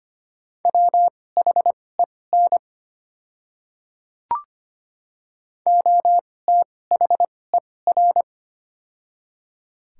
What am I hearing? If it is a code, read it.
Morse code, heard as a keyed tone, about 25 wpm, spaced as for 18 wpm, 690 Hz. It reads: WHEN OTHER
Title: 2x Character Spacing - Top 100 Words - 25wpm